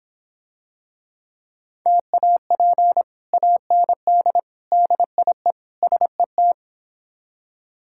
Morse code, heard as a keyed tone, about 26 wpm, 700 Hz. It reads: TAP AND DIE SET